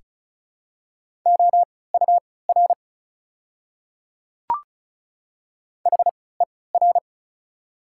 Morse code, heard as a keyed tone, about 34 wpm, spaced as for 20 wpm, 690 Hz. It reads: OUR HER
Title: 3x Character Spacing - Top 500 Words - 35wpm